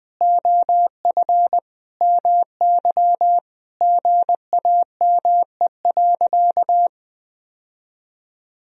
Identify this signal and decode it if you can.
Morse code, heard as a keyed tone, about 21 wpm, 700 Hz. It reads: OF MY GAME.
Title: Sentences from Top 1000 Words - Review Part 2 - 20wpm